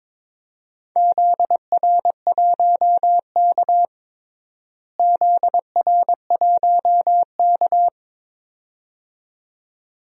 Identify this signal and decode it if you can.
Morse code, heard as a keyed tone, about 22 wpm, 700 Hz. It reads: ZR1K ZR1K
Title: DXCC-100 Call Signs - 4 Characters - Repeated 2x - 22wpm